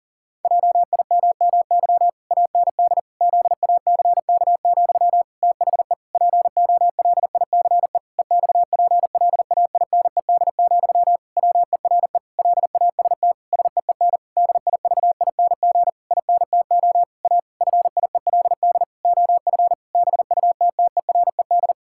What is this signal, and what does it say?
Morse code, heard as a keyed tone, about 40 wpm, 700 Hz. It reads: JIMMY AND ZACK, THE POLICE EXPLAINED, WERE LAST SEEN DIVING INTO A FIELD OF BUTTERED
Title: Pangram Challenge v2 - 40wpm